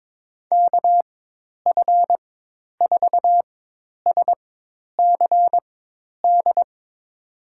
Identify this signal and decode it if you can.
Morse code, heard as a keyed tone, about 21 wpm, spaced as for 10 wpm, 700 Hz. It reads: KF4SCD
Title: USA Call Signs - 6 Characters - 4x Character Spacing - 22wpm